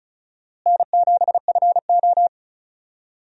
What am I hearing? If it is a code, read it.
Morse code, heard as a keyed tone, about 36 wpm, 700 Hz. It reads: N7FO
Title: USA Call Signs - 4 Characters - Repeated 3x - 35wpm